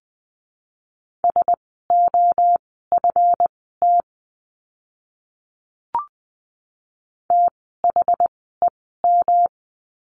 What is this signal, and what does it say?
Morse code, heard as a keyed tone, about 20 wpm, spaced as for 14 wpm, 700 Hz. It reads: SOFT THEM
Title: 139 Lesson 23f - Letter f - Words 20wpm